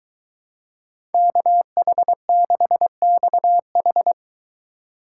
Morse code, heard as a keyed tone, about 23 wpm, 700 Hz. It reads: KH6XH